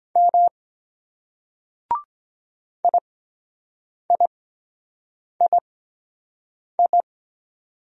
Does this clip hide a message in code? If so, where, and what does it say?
Yes: Morse code from 0.2 to 7.0 s, heard as a keyed tone, about 22 wpm, spaced as for 7 wpm, 700 Hz. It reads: M IIII